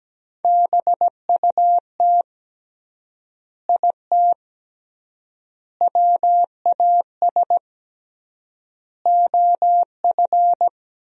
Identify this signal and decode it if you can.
Morse code, heard as a keyed tone, about 17 wpm, 700 Hz. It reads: BUT IT WAS OF
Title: 3x Word-Spacing - Sentences from Top 100 Words - 17wpm